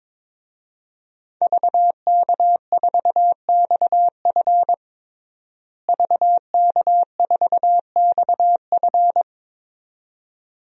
Morse code, heard as a keyed tone, about 22 wpm, 700 Hz. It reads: VK4XF VK4XF
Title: DXCC-100 Call Signs - 5 Characters - Repeated 2x - 22wpm